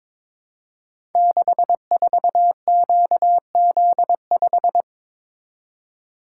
Morse code, heard as a keyed tone, about 22 wpm, 700 Hz. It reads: 64QZ5
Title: Code Groups - 5 Letter:Numbers - 22wpm